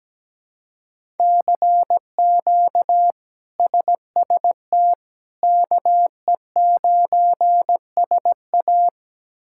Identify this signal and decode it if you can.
Morse code, heard as a keyed tone, about 17 wpm, 700 Hz. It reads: CQ SST KE9SA
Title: K1USN SST QSO Practice - 17wpm